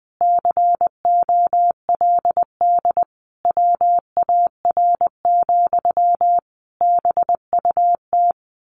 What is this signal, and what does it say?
Morse code, heard as a keyed tone, about 20 wpm, 700 Hz. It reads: COLD WAR, BUT
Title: Sentences from Top 1200 Words 20wpm